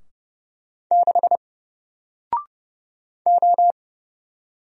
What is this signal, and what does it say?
Morse code, heard as a keyed tone, about 30 wpm, 700 Hz. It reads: 6 O
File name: Single-Letter-Number-30wpm